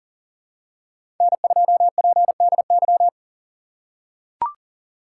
Morse code, heard as a keyed tone, about 40 wpm, 700 Hz. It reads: N2PDY
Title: USA Call Signs - 5 Characters - 4x Character Spacing - 40wpm